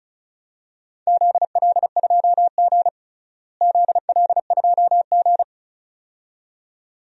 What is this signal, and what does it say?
Morse code, heard as a keyed tone, about 36 wpm, 700 Hz. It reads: ZL2G ZL2G
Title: DXCC-100 Call Signs - 4 Characters - Repeated 2x - 35wpm